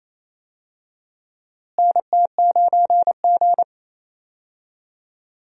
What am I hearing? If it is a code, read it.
Morse code, heard as a keyed tone, about 28 wpm, 700 Hz. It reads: NT9G
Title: USA Call Signs - 4 Characters - Repeated 2x - 28wpm